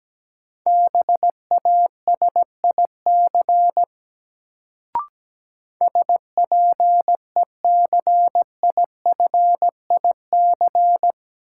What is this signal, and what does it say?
Morse code, heard as a keyed tone, about 17 wpm, 700 Hz. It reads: BASIC SPECIFIC
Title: Sentences from Top 1000 Words - Review Part 3 - New Word Drill - 17wpm